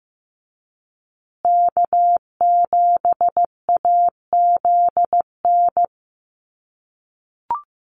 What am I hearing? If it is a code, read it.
Morse code, heard as a keyed tone, about 15 wpm, 700 Hz. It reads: K7AZN
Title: USA Call Signs Repeated 2x 15wpm